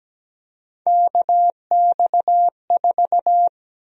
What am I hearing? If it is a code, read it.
Morse code, heard as a keyed tone, about 17 wpm, 700 Hz. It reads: KX4
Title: USA Call Sign Prefixes - 17wpm